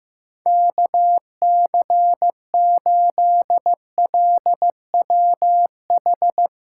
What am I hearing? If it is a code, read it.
Morse code, heard as a keyed tone, about 15 wpm, 700 Hz. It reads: KC8LWH
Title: USA Callsign DE Callsign 15wpm